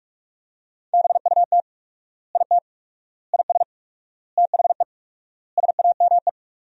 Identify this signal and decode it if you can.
Morse code, heard as a keyed tone, about 46 wpm, 700 Hz. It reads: BUT IT IS THE SAME
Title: 4x Word-Spacing - Sentences from Top 300 Words - 45wpm